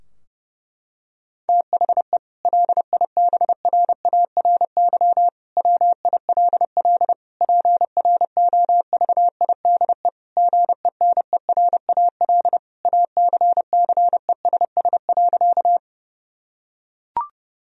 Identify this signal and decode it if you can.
Morse code, heard as a keyed tone, about 32 wpm, 700 Hz. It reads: THE LIBRARY WILL PROVIDE GENERAL ACCESS.
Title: Sentences from Top 1000 Words - Review Part 4 - 30wpm